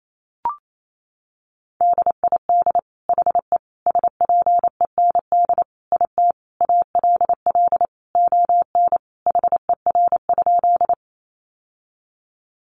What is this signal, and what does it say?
Morse code, heard as a keyed tone, about 28 wpm, 700 Hz. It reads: DID HE SPEND IT ALL ON HER?
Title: Sentences from Top 800 Words 28wpm